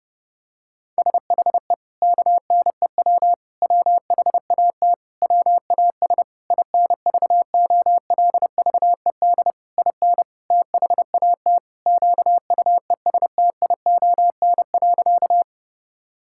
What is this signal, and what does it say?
Morse code, heard as a keyed tone, about 30 wpm, 700 Hz. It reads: SHE KNEW WHAT WAS INVOLVED IN THAT QUESTION.
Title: Sentences from Top 1000 Words - Review Part 5 - 30wpm